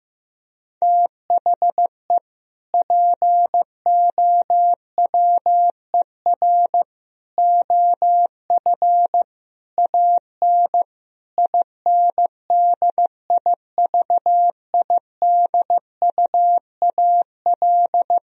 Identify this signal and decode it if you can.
Morse code, heard as a keyed tone, about 15 wpm, 700 Hz. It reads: THE POWER OF AN INDIVIDUAL
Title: Sentences from Top 1000 Words - Review Part 4 - 15wpm